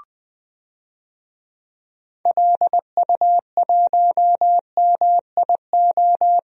Morse code, heard as a keyed tone, about 20 wpm, 700 Hz. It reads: LU1MIO
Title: DXCC-100 Call Signs - 6 Characters - 20wpm